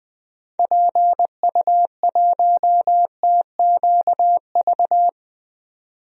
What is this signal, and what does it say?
Morse code, heard as a keyed tone, about 20 wpm, 700 Hz. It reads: PU1TQV
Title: DXCC-100 Call Signs - 6 Characters - 20wpm